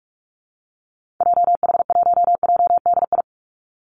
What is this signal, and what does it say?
Morse code, heard as a keyed tone, about 44 wpm, 700 Hz. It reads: JH1JDI
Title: Call Signs - 45wpm